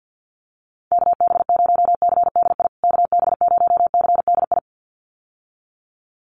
Morse code, heard as a keyed tone, about 50 wpm, 700 Hz. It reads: KB0CDI KB0CDI
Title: USA Call Signs Repeated 2x 50wpm